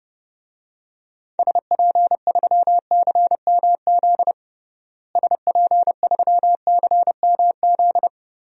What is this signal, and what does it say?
Morse code, heard as a keyed tone, about 32 wpm, 700 Hz. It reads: SP3CMZ SP3CMZ
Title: DXCC-100 Call Signs - 6 Characters - Repeated 2x - 30wpm